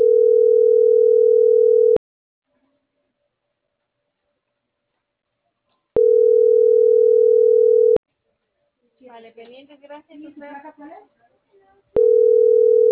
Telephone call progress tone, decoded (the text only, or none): ringback tone